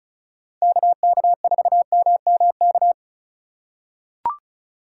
Morse code, heard as a keyed tone, about 36 wpm, 700 Hz. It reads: KK4MMK
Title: USA Call Signs - 4x Character Spacing - 35wpm